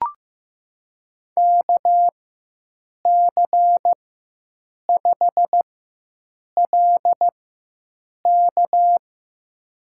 Morse code, heard as a keyed tone, about 15 wpm, spaced as for 7 wpm, 700 Hz. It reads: KC5LK
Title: USA Call Signs - 4x Character Spacing - 15wpm